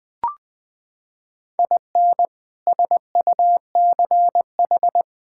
Morse code, heard as a keyed tone, about 21 wpm, 700 Hz. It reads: IN SUCH
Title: Sentences from Top 1000 Words - Review Part 5 - 20wpm